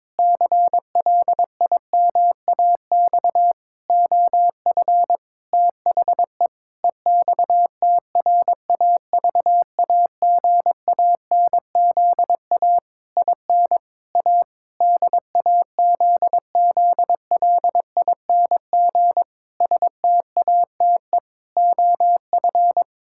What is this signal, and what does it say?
Morse code, heard as a keyed tone, about 22 wpm, 700 Hz. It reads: CLIMAX OF THE EXTRAVAGANZA IN A DAZZLING STATE OF
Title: Pangram Challenge v2 - 22wpm